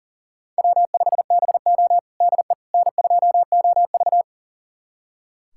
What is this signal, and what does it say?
Morse code, heard as a keyed tone, about 37 wpm, 700 Hz. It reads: W5BO DE N2OV
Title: USA Callsign DE Callsign - 4 Characters - 40wpm